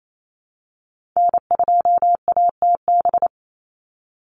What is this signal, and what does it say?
Morse code, heard as a keyed tone, about 28 wpm, 700 Hz. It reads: N2ATB
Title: Call Signs - 28wpm